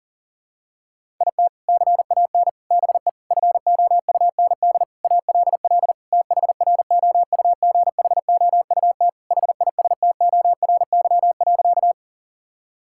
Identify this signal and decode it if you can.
Morse code, heard as a keyed tone, about 40 wpm, 700 Hz. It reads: IT CAN BE FOUND ALL THROUGHOUT HISTORY.
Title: Sentences from Top 1000 Words - Review Part 5 - 40wpm